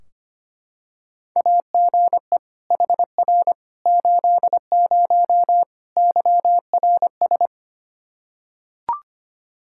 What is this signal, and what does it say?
Morse code, heard as a keyed tone, about 25 wpm, 700 Hz. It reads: AGE HR 80 YRS
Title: K1USN SST QSO Practice 25wpm